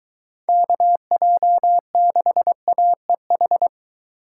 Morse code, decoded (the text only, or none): KJ6AEH